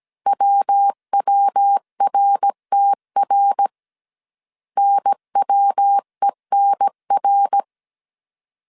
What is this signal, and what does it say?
Morse code, heard as a keyed tone, about 17 wpm, 800 Hz. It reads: WWRTR NWENR